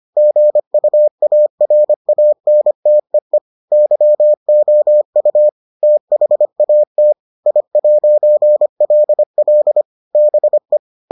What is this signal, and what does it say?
Morse code, heard as a keyed tone, about 25 wpm, 600 Hz. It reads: GUARANTEE YOU THAT I'LL BE